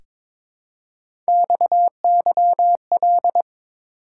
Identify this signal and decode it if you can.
Morse code, heard as a keyed tone, about 23 wpm, 700 Hz. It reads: XYL
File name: Top 90 QSO Elements - 4x Character Spacing - 22wpm